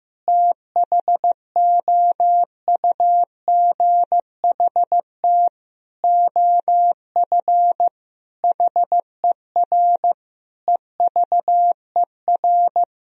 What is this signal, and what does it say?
Morse code, heard as a keyed tone, about 15 wpm, 700 Hz. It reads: THOUGHT OF HER EVER
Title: Sentences from Top 1000 Words - Review Part 1 - 15wpm